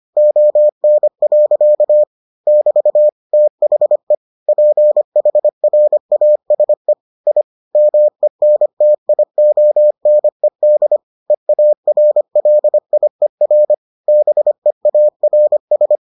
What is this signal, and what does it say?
Morse code, heard as a keyed tone, about 25 wpm, 600 Hz. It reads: ON. <BT> THE PHRASE I MENTIONED EARLIER BEARS